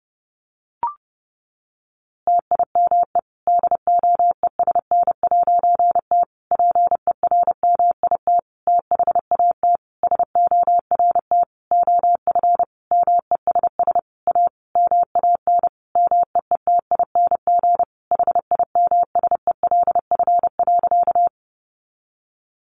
Morse code, heard as a keyed tone, about 30 wpm, 700 Hz. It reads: TIME DOESN'T PERMIT THAT SORT OF MESS A MAN MEETING HIMSELF.
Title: A Sound Of Thunder 30wpm